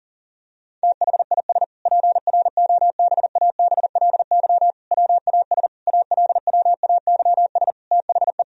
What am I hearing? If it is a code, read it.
Morse code, heard as a keyed tone, about 40 wpm, 700 Hz. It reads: THIS PROBABLY WAS ALWAYS THE
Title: Sentences from Top 1000 Words - Review Part 4 - 40wpm